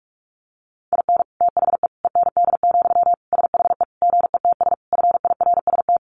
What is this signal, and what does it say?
Morse code, heard as a keyed tone, about 44 wpm, 700 Hz. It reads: IN THE END, SHE GETS FIRST
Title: Additional Easy Phrases 45wpm